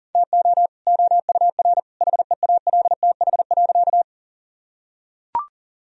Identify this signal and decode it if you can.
Morse code, heard as a keyed tone, about 40 wpm, 700 Hz. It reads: TO OUR HEALTH.
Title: Sentences from Top 1000 Words - Review Part 3 - 40wpm